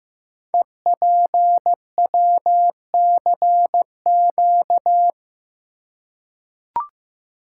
Morse code, heard as a keyed tone, about 15 wpm, 700 Hz. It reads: EPWCQ
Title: Code Groups - 5 Letter:Numbers - 15wpm